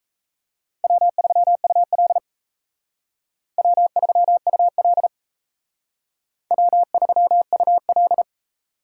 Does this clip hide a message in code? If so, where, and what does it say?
Yes: Morse code from 0.8 to 8.2 s, heard as a keyed tone, about 37 wpm, 700 Hz. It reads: W3UL W3UL W3UL